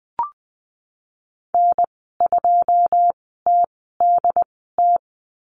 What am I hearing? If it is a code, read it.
Morse code, heard as a keyed tone, about 20 wpm, spaced as for 15 wpm, 700 Hz. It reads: N2TDT
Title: 128 Lesson 21g - Number 6 - Call Signs 20wpm